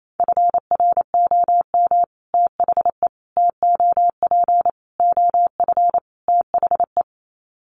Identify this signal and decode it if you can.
Morse code, heard as a keyed tone, about 28 wpm, 700 Hz. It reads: FROM THE TOP OF THE
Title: Sets of 5 Words Encore 28wpm